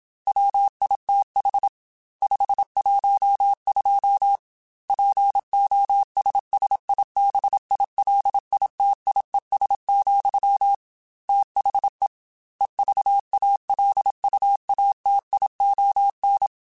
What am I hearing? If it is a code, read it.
Morse code, heard as a keyed tone, about 27 wpm, 800 Hz. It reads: WITH 512 POSSIBILITIES, THE EVALUATION
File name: Technical Subjects / Technical Subjects-mp3s / AND Gate @25